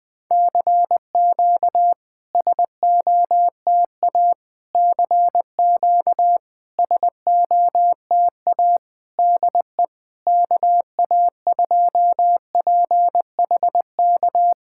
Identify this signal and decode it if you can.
Morse code, heard as a keyed tone, about 20 wpm, 700 Hz. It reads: CQ SOTA CQ SOTA DE KA2PHK
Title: SOTA QSO Practice 20wpm